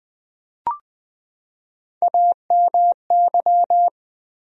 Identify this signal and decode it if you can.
Morse code, heard as a keyed tone, about 20 wpm, 700 Hz. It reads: AMY
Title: USA Common Names - Max 4 Letters - 20wpm